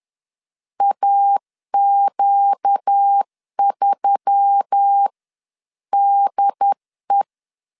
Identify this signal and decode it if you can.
Morse code, heard as a keyed tone, about 10 wpm, spaced as for 9 wpm, 800 Hz. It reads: AQ3DE
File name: CW / DET / 1 6 1500C S C 10wpm 4252